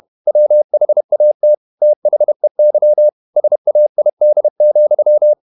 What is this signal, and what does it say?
Morse code, heard as a keyed tone, about 31 wpm, 600 Hz. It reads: WHAT THEY SAID,